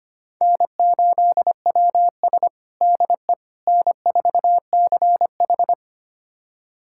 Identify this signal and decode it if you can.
Morse code, heard as a keyed tone, about 25 wpm, 700 Hz. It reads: N8WS DE N4CH
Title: USA Callsign DE Callsign - 4 Characters - 25wpm